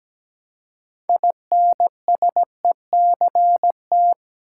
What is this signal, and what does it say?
Morse code, heard as a keyed tone, about 17 wpm, 700 Hz. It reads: INSECT
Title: Six Letter Words - 17wpm